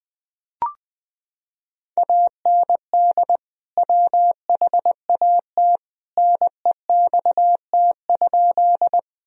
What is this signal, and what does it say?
Morse code, heard as a keyed tone, about 20 wpm, 700 Hz. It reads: AND WHAT NEXT?
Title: Sentences from Top 1000 Words - Review Part 1 - 20wpm